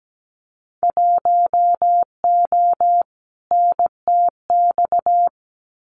Morse code, heard as a keyed tone, about 17 wpm, 700 Hz. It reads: 1O NTX
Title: Winter Field Day Exchanges Encore 17wpm